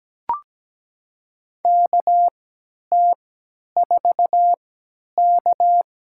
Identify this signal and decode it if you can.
Morse code, heard as a keyed tone, about 17 wpm, spaced as for 10 wpm, 700 Hz. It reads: KT4K
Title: USA Call Signs - 4 Characters - 3x Character Spacing - 17wpm